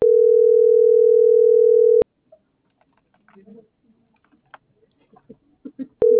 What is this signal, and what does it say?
Ringback tone (North America), 440 + 480 Hz.